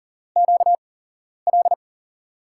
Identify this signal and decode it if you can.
Morse code, heard as a keyed tone, about 40 wpm, spaced as for 11 wpm, 700 Hz. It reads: QL
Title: USA Call Signs - 5 Characters - 8x Character Spacing - 40wpm